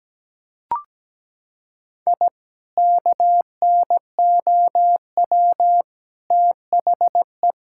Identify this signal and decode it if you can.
Morse code, heard as a keyed tone, about 17 wpm, 700 Hz. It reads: I KNOW THE
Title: Sentences from Top 1000 Words - Review Part 5 - 17wpm